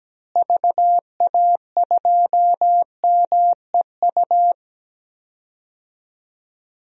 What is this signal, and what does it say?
Morse code, heard as a keyed tone, about 17 wpm, 700 Hz. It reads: VA2MEU
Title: DXCC-100 Call Signs - 6 Characters - Repeated 3x - 17wpm